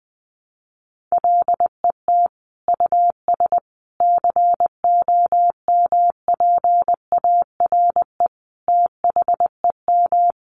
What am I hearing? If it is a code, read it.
Morse code, heard as a keyed tone, about 19 wpm, 700 Hz. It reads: LET US COMPARE THEM
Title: Sentences from Top 1300 Words 20wpm